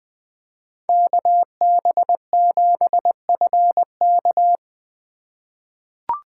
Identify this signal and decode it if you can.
Morse code, heard as a keyed tone, about 20 wpm, 700 Hz. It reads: KB7FK
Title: USA Call Signs - 5 Characters - 4x Character Spacing - 20wpm